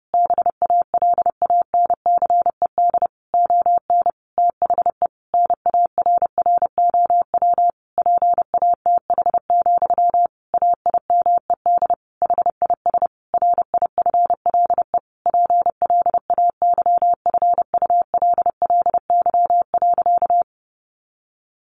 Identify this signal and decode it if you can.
Morse code, heard as a keyed tone, about 30 wpm, 700 Hz. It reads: BALANCED ON THE NARROW PATH, AIMED HIS RIFLE PLAYFULLY.
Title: A Sound Of Thunder 30wpm